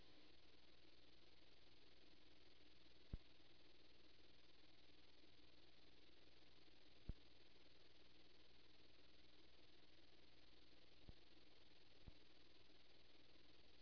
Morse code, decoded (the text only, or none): none